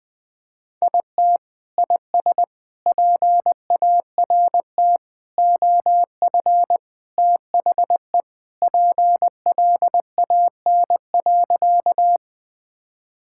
Morse code, heard as a keyed tone, about 20 wpm, 700 Hz. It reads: IT IS PART OF THE PLAN.
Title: Sentences with 4-Word Phrases - 20wpm